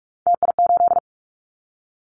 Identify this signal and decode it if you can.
Morse code, heard as a keyed tone, about 44 wpm, 700 Hz. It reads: TI8
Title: Callsign Prefixes for Top 100 DXCC - 45wpm